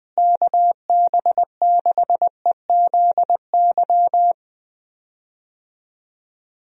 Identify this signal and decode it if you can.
Morse code, heard as a keyed tone, about 20 wpm, 700 Hz. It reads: KB6EZY
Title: USA Call Signs - 6 Characters - Repeated 3x - 20wpm